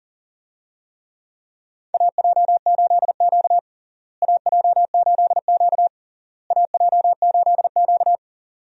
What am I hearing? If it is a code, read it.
Morse code, heard as a keyed tone, about 40 wpm, 700 Hz. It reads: AJ8Q AJ8Q AJ8Q
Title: USA Call Signs - 4 Characters - Repeated 3x - 40wpm